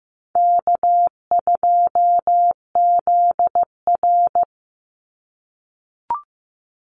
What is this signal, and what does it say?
Morse code, heard as a keyed tone, about 15 wpm, 700 Hz. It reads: K2ZR
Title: Easy Call Signs - 15wpm